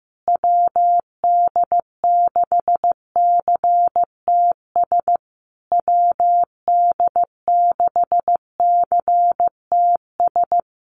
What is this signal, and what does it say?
Morse code, heard as a keyed tone, about 15 wpm, 700 Hz. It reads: WD6CTS WD6CTS
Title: USA Call Signs Repeated 3x 15wpm